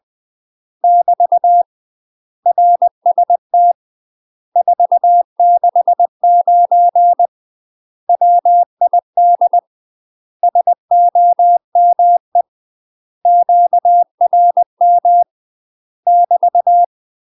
Morse code, heard as a keyed tone, about 20 wpm, 700 Hz. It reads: <BT> RST 469 WID SOME QRM <BT>
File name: UK MASTER QSOs TRAINING / 20WPM / UK MASTER QSOs 0002